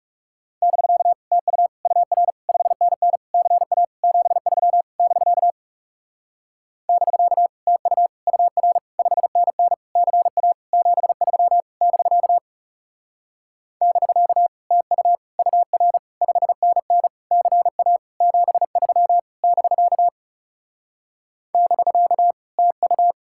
Morse code, heard as a keyed tone, about 40 wpm, 700 Hz. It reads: <BK> TU UR 5NN CA 73 <BK> <BK> TU UR 5NN CA 73 <BK> <BK> TU UR 5NN CA 73 <BK> <BK> TU